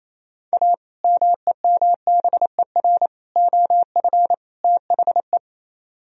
Morse code, heard as a keyed tone, about 37 wpm, 700 Hz. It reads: A MEMBER OF THE